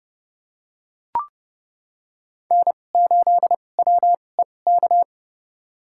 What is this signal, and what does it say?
Morse code, heard as a keyed tone, about 30 wpm, spaced as for 22 wpm, 700 Hz. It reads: N8WEK
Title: USA Call Signs - 5 Characters - 2x Character Spacing - 30wpm